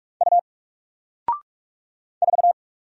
Morse code, heard as a keyed tone, about 44 wpm, 700 Hz. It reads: U 4
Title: Single Character (Letter:Number:Punctuation:Prosign) Alt 45wpm